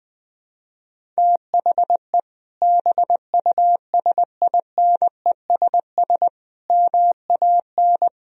Morse code, heard as a keyed tone, about 20 wpm, 700 Hz. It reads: THE BUSINESS MAN
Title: Sentences from Top 1000 Words - Review Part 1 - 20wpm